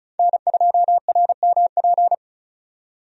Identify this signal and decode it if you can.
Morse code, heard as a keyed tone, about 36 wpm, 700 Hz. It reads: N2RMP
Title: USA Call Signs - 5 Characters - 35wpm